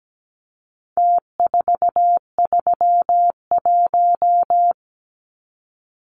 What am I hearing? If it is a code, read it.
Morse code, heard as a keyed tone, about 17 wpm, 700 Hz. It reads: T431
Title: Numbers - Contest Style - 4-Digit 17wpm